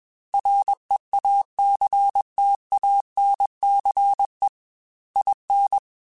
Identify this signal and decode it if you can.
Morse code, heard as a keyed tone, about 21 wpm, 800 Hz. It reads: REACTANCE IN
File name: Technical Subjects / Technical Subjects-mp3s / XC @20